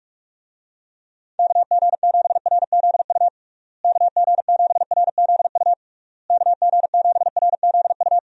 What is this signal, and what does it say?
Morse code, heard as a keyed tone, about 46 wpm, 700 Hz. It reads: KG7RZU KG7RZU KG7RZU
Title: USA Call Signs - 6 Characters - Repeated 3x - 45wpm